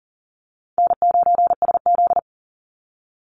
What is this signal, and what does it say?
Morse code, heard as a keyed tone, about 40 wpm, 700 Hz. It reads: N9SZ
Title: Easy Call Signs Repeated 2x - 40wpm